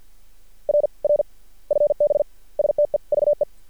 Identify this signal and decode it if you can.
Morse code, heard as a keyed tone, about 52 wpm, spaced as for 46 wpm, 590 Hz. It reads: R R FB STEVE
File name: archived / CRSnet